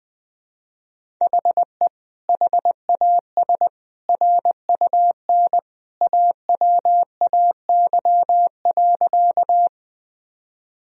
Morse code, heard as a keyed tone, about 20 wpm, 700 Hz. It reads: HE HAS RUN AWAY.
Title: Sentences from Top 1000 Words - Review Part 1 - 20wpm